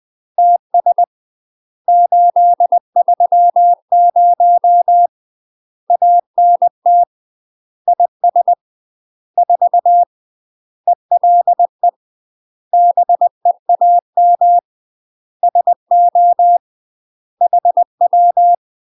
Morse code, heard as a keyed tone, about 20 wpm, 700 Hz. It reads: TS 830 ANT IS 4 ELE BEAM SO HW
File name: UK MASTER QSOs TRAINING / 20WPM / UK MASTER QSOs 0035